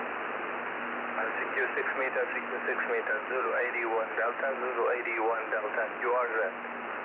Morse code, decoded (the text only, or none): none